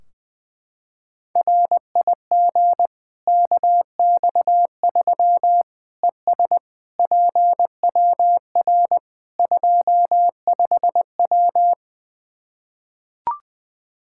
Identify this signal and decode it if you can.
Morse code, heard as a keyed tone, about 20 wpm, 700 Hz. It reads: RIG KX3 ES PWR 25W
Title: K1USN SST QSO Practice 20wpm